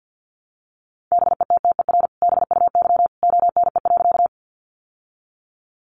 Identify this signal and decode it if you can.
Morse code, heard as a keyed tone, about 50 wpm, 700 Hz. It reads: BETTER BUY ONE.
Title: Sentences from Top 700 Words 50wpm